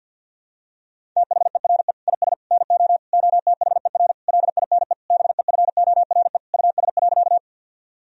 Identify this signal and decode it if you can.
Morse code, heard as a keyed tone, about 50 wpm, 700 Hz. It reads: THERE IS NO OTHER LINE BEFORE US.
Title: Sentences from Top 1000 Words - Review Part 1 - 50wpm